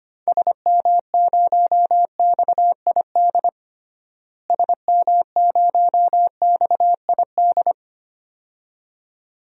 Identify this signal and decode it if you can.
Morse code, heard as a keyed tone, about 25 wpm, 700 Hz. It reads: SM0XID SM0XID
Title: DXCC-100 Call Signs - 6 Characters - Repeated 2x - 25wpm